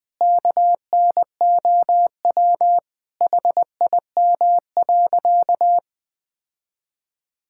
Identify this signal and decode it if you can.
Morse code, heard as a keyed tone, about 20 wpm, 700 Hz. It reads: KNOW HIM.
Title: Sentences with 3-Word Phrases - 20wpm